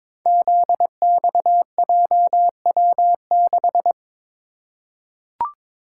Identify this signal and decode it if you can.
Morse code, heard as a keyed tone, about 22 wpm, 700 Hz. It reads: ZXJW6
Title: Code Groups - 5 Letter:Numbers - 22wpm